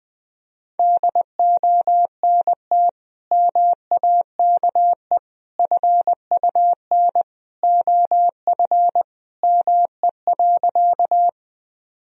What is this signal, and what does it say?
Morse code, heard as a keyed tone, about 20 wpm, 700 Hz. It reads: DONT MAKE FUN OF ME.
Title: Sentences from Top 1000 Words - Review Part 4 - 20wpm